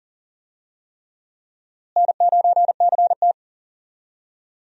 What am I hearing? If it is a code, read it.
Morse code, heard as a keyed tone, about 37 wpm, 700 Hz. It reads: N9CT
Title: USA Call Signs - 4 Characters - Repeated 3x - 40wpm